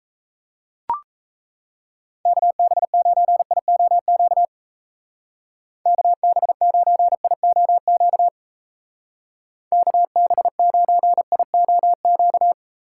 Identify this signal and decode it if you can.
Morse code, heard as a keyed tone, about 35 wpm, 700 Hz. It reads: KB9IOQ KB9IOQ KB9IOQ